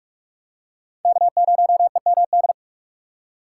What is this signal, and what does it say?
Morse code, heard as a keyed tone, about 36 wpm, 700 Hz. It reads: K0EMD
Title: Speed Racing - USA 1x3 Call Signs 30wpm